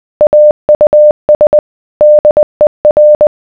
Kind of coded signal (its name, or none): Morse code